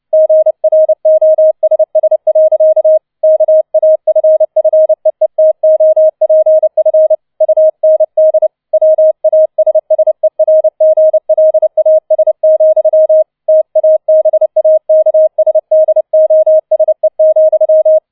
Morse code, heard as a keyed tone, about 29 wpm, 630 Hz. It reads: GROSS. KAFFEETOPF UND WASSERGLAS, TABAKSDOSE,